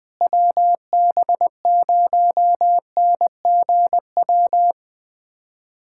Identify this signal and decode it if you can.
Morse code, heard as a keyed tone, about 20 wpm, 700 Hz. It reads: WB0NGW
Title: USA Callsign DE Callsign - 6 Characters - 20wpm